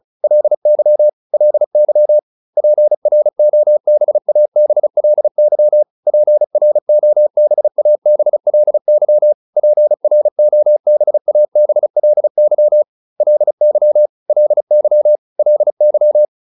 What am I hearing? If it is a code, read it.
Morse code, heard as a keyed tone, about 34 wpm, 600 Hz. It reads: LY LY PROBABLY PROBABLY PROBABLY LY LY LY